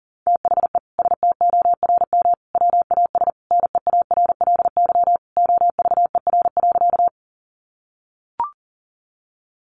Morse code, heard as a keyed tone, about 40 wpm, 700 Hz. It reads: THE STORM WAS NEARLY OVER.
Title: Sentences from Top 1500 Words 40wpm